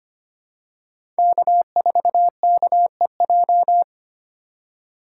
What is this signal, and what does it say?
Morse code, heard as a keyed tone, about 25 wpm, 700 Hz. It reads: K4KEJ
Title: USA Call Signs - 4x Character Spacing - 25wpm